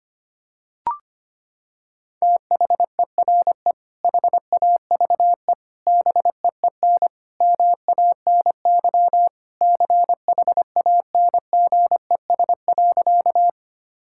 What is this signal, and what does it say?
Morse code, heard as a keyed tone, about 25 wpm, 700 Hz. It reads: THERE HAVE BEEN MANY CHANGES.
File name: practice / 25wpm / Sentences with 3-Word Phrases - 25wpm